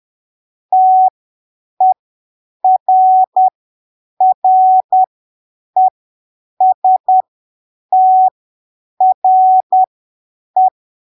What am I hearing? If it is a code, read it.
Morse code, heard as a keyed tone, about 10 wpm, spaced as for 7 wpm, 750 Hz. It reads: TERRESTRE
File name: arquivos / texto / texto 10 10ppm